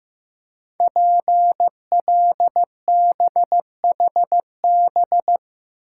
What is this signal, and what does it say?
Morse code, heard as a keyed tone, about 15 wpm, 700 Hz. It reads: PLBHB
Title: Code Groups - 5 Letter:Numbers - 15wpm